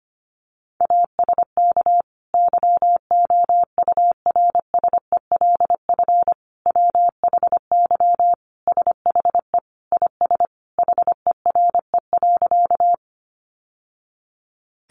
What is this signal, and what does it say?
Morse code, heard as a keyed tone, about 25 wpm, 700 Hz. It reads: ASK YOURSELF WHY SHE IS HERE.
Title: Sentences from Top 600 Words 25wpm